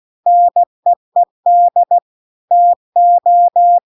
Morse code, heard as a keyed tone, about 16 wpm, 700 Hz. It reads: NEED TO